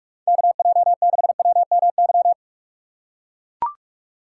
Morse code, heard as a keyed tone, about 42 wpm, 700 Hz. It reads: KJ6WMY
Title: USA Call Signs - 6 Characters - 3x Character Spacing - 45wpm